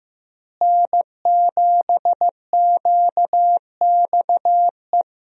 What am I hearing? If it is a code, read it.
Morse code, heard as a keyed tone, about 15 wpm, 700 Hz. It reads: N7QXE
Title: USA Call Signs - 5 Characters - Repeated 2x - 15wpm